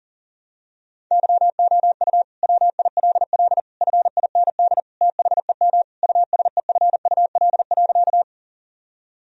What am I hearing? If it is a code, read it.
Morse code, heard as a keyed tone, about 37 wpm, 700 Hz. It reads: YOU WILL FIND THEM USEFUL.
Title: Sentences from Top 1000 Words - Review Part 4 - 40wpm